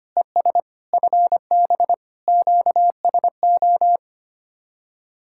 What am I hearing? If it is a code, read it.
Morse code, heard as a keyed tone, about 29 wpm, 700 Hz. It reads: ES FB QSO
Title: Speed-Racing Common QSO Phrases Encore - 25wpm